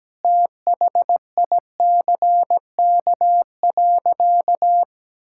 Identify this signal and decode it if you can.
Morse code, heard as a keyed tone, about 17 wpm, 700 Hz. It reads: THICK.
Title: Sentences from Top 1600 Words 17wpm